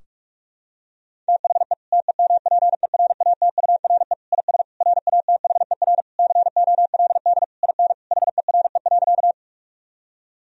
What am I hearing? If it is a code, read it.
Morse code, heard as a keyed tone, about 46 wpm, 700 Hz. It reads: THE TEMPERATURE IS RATHER COLD IN HERE.
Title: Sentences from Top 1000 Words - Review Part 4 - 45wpm